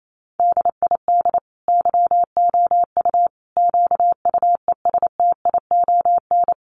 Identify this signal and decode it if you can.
Morse code, heard as a keyed tone, about 28 wpm, 700 Hz. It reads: DID YOU QUESTION
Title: Sentences from Top 800 Words 28wpm